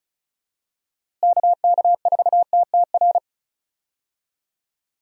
Morse code, heard as a keyed tone, about 36 wpm, 700 Hz. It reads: KK4TTR
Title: USA Call Signs - 6 Characters - Repeated 2x - 35wpm